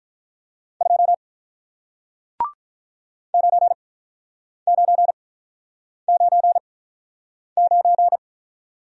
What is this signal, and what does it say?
Morse code, heard as a keyed tone, about 48 wpm, spaced as for 9 wpm, 700 Hz. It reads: 2 9999